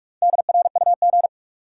Morse code, heard as a keyed tone, about 45 wpm, 700 Hz. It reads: DRUG
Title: Sets of 5 Words - New Word Drill - 45wpm